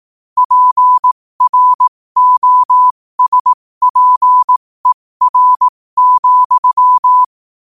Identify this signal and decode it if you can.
Morse code, heard as a keyed tone, about 18 wpm, spaced as for 15 wpm, 1000 Hz. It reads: PROSPER,
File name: qotdfiles / 15wpm